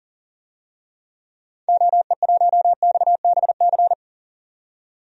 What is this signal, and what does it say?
Morse code, heard as a keyed tone, about 39 wpm, 700 Hz. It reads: OE1XBC
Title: DXCC-100 Call Signs - 6 Characters - 40wpm